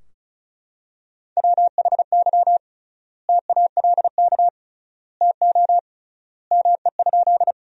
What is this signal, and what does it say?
Morse code, heard as a keyed tone, about 34 wpm, 700 Hz. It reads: WHY TALK TO ME?
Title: 3x Word-Spacing - Sentences from Top 400 Words - 35wpm